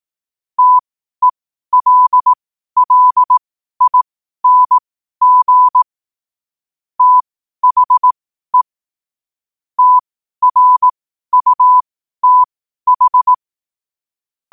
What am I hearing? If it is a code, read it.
Morse code, heard as a keyed tone, about 18 wpm, spaced as for 12 wpm, 1000 Hz. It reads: TELLING THE TRUTH
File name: qotdfiles / 12wpm